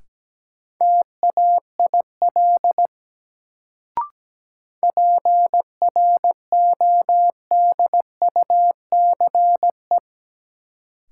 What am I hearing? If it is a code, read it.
Morse code, heard as a keyed tone, about 17 wpm, 700 Hz. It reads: TAIL PRODUCE
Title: Sentences from Top 1000 Words - Review Part 3 - New Word Drill - 17wpm